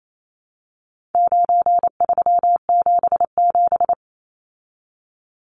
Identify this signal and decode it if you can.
Morse code, heard as a keyed tone, about 28 wpm, 700 Hz. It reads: 9377
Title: Numbers - 4-Digit 28wpm